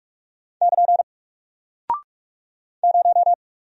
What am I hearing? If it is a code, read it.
Morse code, heard as a keyed tone, about 46 wpm, 700 Hz. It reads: <KN> 0
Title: Single Character (Letter:Number:Punctuation:Prosign) Alt 45wpm